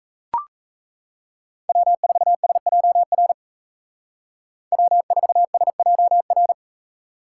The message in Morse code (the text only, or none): W4SJR W4SJR